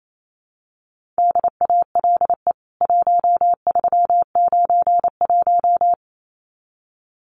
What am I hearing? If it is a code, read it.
Morse code, heard as a keyed tone, about 28 wpm, 700 Hz. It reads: DALE 1391
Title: CWT Exchange 28wpm